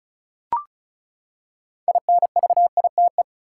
Morse code, heard as a keyed tone, about 36 wpm, 700 Hz. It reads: INVITE
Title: Six Letter Words - 35wpm